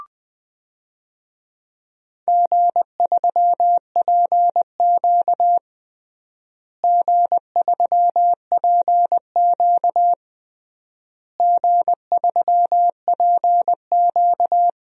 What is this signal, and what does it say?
Morse code, heard as a keyed tone, about 19 wpm, 700 Hz. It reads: G3PQ G3PQ G3PQ
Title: DXCC-100 Call Signs - 4 Characters - Repeated 3x - 20wpm